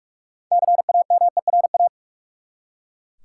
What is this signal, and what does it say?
Morse code, heard as a keyed tone, about 42 wpm, 700 Hz. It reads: CAMERA
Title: Six Letter Words - 45wpm